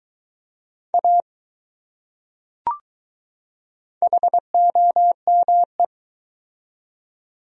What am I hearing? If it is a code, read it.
Morse code, heard as a keyed tone, about 20 wpm, 700 Hz. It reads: A HOME